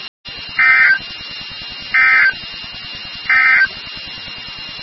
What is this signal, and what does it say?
Emergency Alert System (SAME) tones, heard as harsh buzzing bursts. It is heard from 0.5 to 3.7 s.
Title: NOAA Weather Radio WXK36